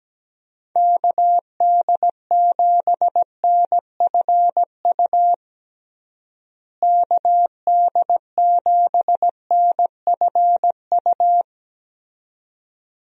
Morse code, heard as a keyed tone, about 17 wpm, 700 Hz. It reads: KD7NFU KD7NFU
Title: USA Call Signs - 6 Characters - Repeated 2x - 17wpm